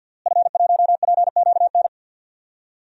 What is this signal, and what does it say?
Morse code, heard as a keyed tone, about 51 wpm, 700 Hz. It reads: F1PQN
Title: DXCC-100 Call Signs - 5 Characters - Repeated 3x - 50wpm